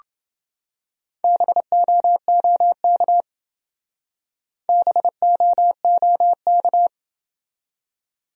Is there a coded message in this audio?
Yes: Morse code from 1.2 to 6.9 s, heard as a keyed tone, about 28 wpm, 700 Hz. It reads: BOOK BOOK